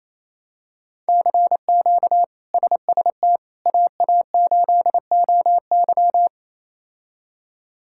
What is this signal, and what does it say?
Morse code, heard as a keyed tone, about 28 wpm, 700 Hz. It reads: CQ SST AA8OY
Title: K1USN SST QSO Practice - 28wpm